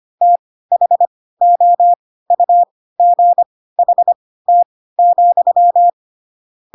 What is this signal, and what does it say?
Morse code, heard as a keyed tone, about 25 wpm, spaced as for 16 wpm, 700 Hz. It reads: THOUGHT,
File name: Pinocchio / Pinocchio ch7 10 25 ik1hge